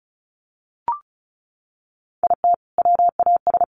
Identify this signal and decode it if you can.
Morse code, heard as a keyed tone, about 34 wpm, 700 Hz. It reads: IT WAS
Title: Sentences from Top 1100 Words 35wpm